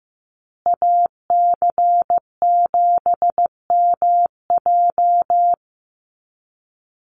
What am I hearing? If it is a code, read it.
Morse code, heard as a keyed tone, about 15 wpm, 700 Hz. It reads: AC7MJ
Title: USA Call Signs Repeated 2x 15wpm